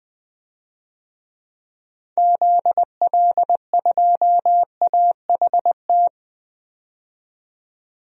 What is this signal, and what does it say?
Morse code, heard as a keyed tone, about 20 wpm, 700 Hz. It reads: ZL2AHT
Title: DXCC-100 Call Signs - 6 Characters - 20wpm